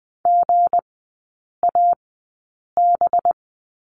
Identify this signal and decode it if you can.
Morse code, heard as a keyed tone, about 20 wpm, spaced as for 8 wpm, 700 Hz. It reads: GAB
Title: 184 Lesson 30e - Letter b - Three Characters 20wpm